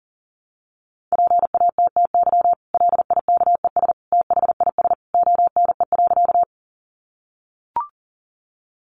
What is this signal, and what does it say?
Morse code, heard as a keyed tone, about 40 wpm, 700 Hz. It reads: PATTY LIKES THIS ONE.
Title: Additional Easy Phrases 40wpm